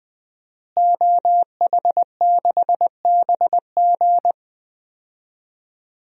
Morse code, heard as a keyed tone, about 20 wpm, 700 Hz. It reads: OH6BG
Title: DXCC-100 Call Signs - 5 Characters - Repeated 2x - 20wpm